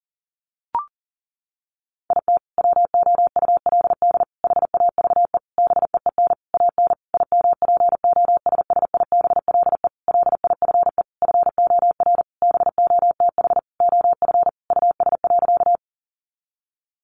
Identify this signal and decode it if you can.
Morse code, heard as a keyed tone, about 40 wpm, 700 Hz. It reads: IT WOULD HAVE BEEN AN IMPOSSIBLE LIFE FOR BOTH OF US.
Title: Sentences from Top 1100 Words 40wpm